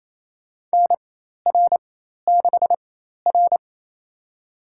Morse code, heard as a keyed tone, about 27 wpm, spaced as for 13 wpm, 700 Hz. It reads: NR6R
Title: USA Call Signs - 4x Character Spacing - 28wpm